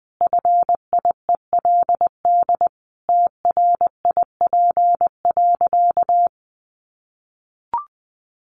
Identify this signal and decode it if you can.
Morse code, heard as a keyed tone, about 20 wpm, 700 Hz. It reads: FIELD TRIP.
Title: Additional Easy Phrases 20wpm